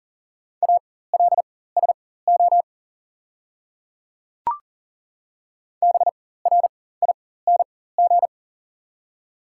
Morse code, heard as a keyed tone, about 39 wpm, spaced as for 18 wpm, 700 Hz. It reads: ALSO BRING
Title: 4x Character Spacing - Top 500 Words - 40wpm